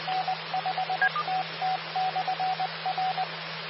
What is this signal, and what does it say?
Morse code, heard as a keyed tone, about 20 wpm, 750 Hz. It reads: N5TT/R
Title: Austin, TX Amateur Repeaters